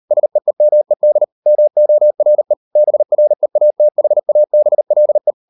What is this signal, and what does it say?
Morse code, heard as a keyed tone, about 38 wpm, 600 Hz. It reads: SEEMED MORE BREATHABLE